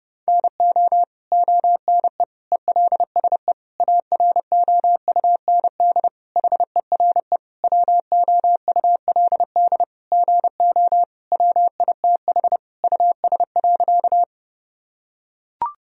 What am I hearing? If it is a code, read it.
Morse code, heard as a keyed tone, about 30 wpm, 700 Hz. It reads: NO ONE ELSE AROUND HERE WOULD GO WITH US.
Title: Sentences from Top 1000 Words - Review Part 3 - 30wpm